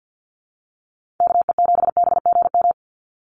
Morse code, heard as a keyed tone, about 48 wpm, 700 Hz. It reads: KE7BGM
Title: NA SOTA Activator Callsigns 50wpm